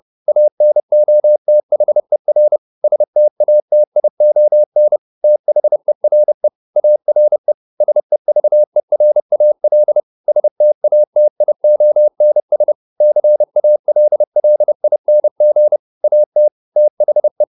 Morse code, heard as a keyed tone, about 30 wpm, 600 Hz. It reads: ANOTHER STATION THERE ARE SEVERAL STATIONS CALLING AT THE